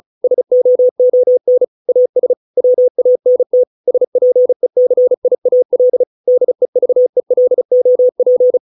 Morse code, heard as a keyed tone, about 34 wpm, 500 Hz. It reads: SOON AS WANT SPECIAL DEVELOP